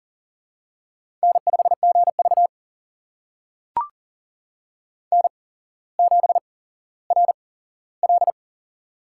Morse code, heard as a keyed tone, about 40 wpm, spaced as for 17 wpm, 700 Hz. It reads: N5GV N7RL
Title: USA Call Signs - 4 Characters - 8x Character Spacing - 40wpm